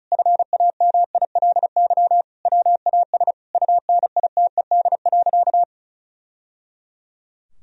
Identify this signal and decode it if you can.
Morse code, heard as a keyed tone, about 34 wpm, 700 Hz. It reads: FAMILY WAS UNITED.
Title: Sentences from Top 1000 Words - Review Part 4 - 35wpm